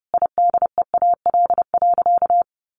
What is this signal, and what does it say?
Morse code, heard as a keyed tone, about 30 wpm, 700 Hz. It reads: IDEAL.
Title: Sentences from Top 1400 Words 30wpm